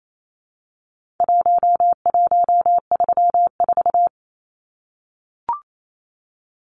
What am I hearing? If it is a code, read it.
Morse code, heard as a keyed tone, about 28 wpm, 700 Hz. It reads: 1134
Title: Numbers - Contest Style - 4-Digit 28wpm